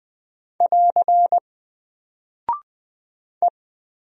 Morse code, heard as a keyed tone, about 20 wpm, 700 Hz. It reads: <AR> E
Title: Single Character (Letter:Number:Punctuation:Prosign) 20wpm